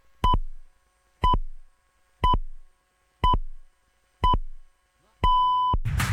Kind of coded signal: radio time signal